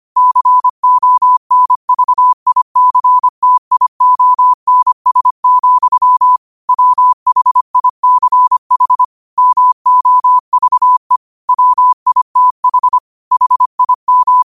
Morse code, heard as a keyed tone, about 25 wpm, 1000 Hz. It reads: CONVICTIONS, WHICH MOVE WITH HIM
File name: qotdfiles / 25wpm